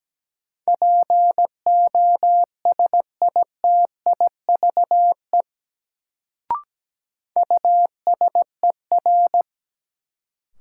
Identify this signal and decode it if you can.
Morse code, heard as a keyed tone, about 17 wpm, 700 Hz. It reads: POSITIVE USER
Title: Sentences from Top 1000 Words - Review Part 5 - New Word Drill - 17wpm